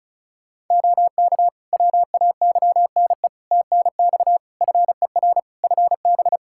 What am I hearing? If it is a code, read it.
Morse code, heard as a keyed tone, about 36 wpm, 700 Hz. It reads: OK WAYNE TNX FER FB
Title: Practice QSO - Exchange 4 - Ending - 35wpm